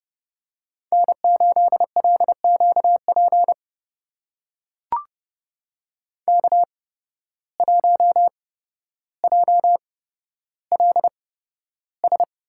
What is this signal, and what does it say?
Morse code, heard as a keyed tone, about 30 wpm, spaced as for 13 wpm, 700 Hz. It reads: N8LQP K1JLS
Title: USA Call Signs - 5 Characters - 8x Character Spacing - 30wpm